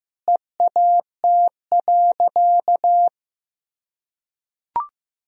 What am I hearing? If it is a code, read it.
Morse code, heard as a keyed tone, about 15 wpm, 700 Hz. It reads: EAT.
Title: Sentences from Top 300 Words and Abbreviations 15wpm